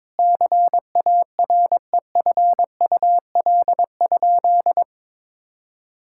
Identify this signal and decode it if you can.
Morse code, heard as a keyed tone, about 21 wpm, 700 Hz. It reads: CAREFUL?
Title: Sentences from Top 1600 Words 22wpm